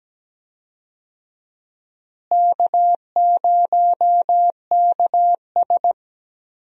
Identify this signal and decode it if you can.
Morse code, heard as a keyed tone, about 17 wpm, 700 Hz. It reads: K0KS
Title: USA Call Signs - 4 Characters - Repeated 2x - 17wpm